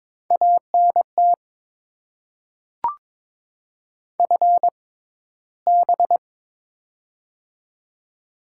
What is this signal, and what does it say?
Morse code, heard as a keyed tone, about 22 wpm, spaced as for 13 wpm, 690 Hz. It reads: ANT FB
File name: Top 20 QSO Elements - 6x Character Spacing - 22wpm